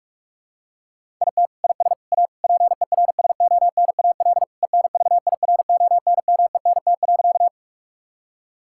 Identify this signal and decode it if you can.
Morse code, heard as a keyed tone, about 41 wpm, 700 Hz. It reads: IT IS A PERSONAL ENVIRONMENT.
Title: Sentences from Top 1000 Words - Review Part 5 - 45wpm